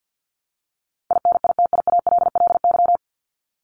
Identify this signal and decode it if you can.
Morse code, heard as a keyed tone, about 50 wpm, 700 Hz. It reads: INITIALLY
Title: Top 1251-1500 Words 50wpm